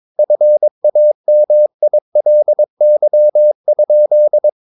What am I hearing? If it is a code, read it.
Morse code, heard as a keyed tone, about 22 wpm, 600 Hz. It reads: FAMILY?
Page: Romanen Never af Ken Follet på engelsk.